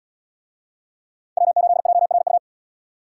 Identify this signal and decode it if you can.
Morse code, heard as a keyed tone, about 70 wpm, 700 Hz. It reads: W8JRU